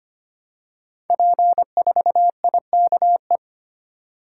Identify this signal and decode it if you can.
Morse code, heard as a keyed tone, about 25 wpm, 700 Hz. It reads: P4IKE
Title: Code Groups - 5 Letter:Numbers - 25wpm